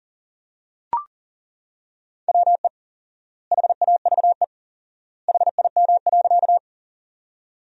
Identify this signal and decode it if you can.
Morse code, heard as a keyed tone, about 37 wpm, 700 Hz. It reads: WE HAVE HIM.
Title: 4x Word-Spacing - Sentences from Top 100 Words - 40wpm